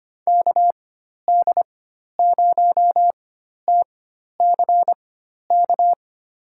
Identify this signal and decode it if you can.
Morse code, heard as a keyed tone, about 25 wpm, spaced as for 12 wpm, 700 Hz. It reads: KD0TCK
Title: USA Call Signs - 4x Character Spacing - 25wpm